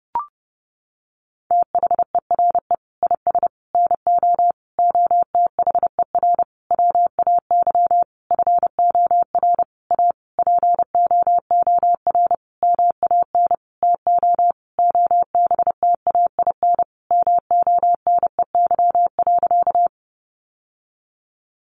Morse code, heard as a keyed tone, about 30 wpm, 700 Hz. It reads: THERE IS NO OTHER WAY FOR A POOR MAN TO OBTAIN MONEY.
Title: Sentences from Top 1300 Words 30wpm